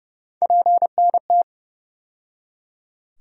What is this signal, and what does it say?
Morse code, heard as a keyed tone, about 30 wpm, 700 Hz. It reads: PNT
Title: USA Call Sign Postfixes - 3 Letters 30wpm